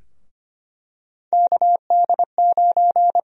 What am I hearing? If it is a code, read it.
Morse code, heard as a keyed tone, about 25 wpm, 700 Hz. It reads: KD9
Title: USA Call Sign Prefixes - Repeated 2x - 25wpm